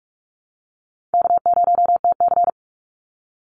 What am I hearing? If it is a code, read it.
Morse code, heard as a keyed tone, about 44 wpm, 700 Hz. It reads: K0TC
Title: Call Signs Repeated 3x - 45wpm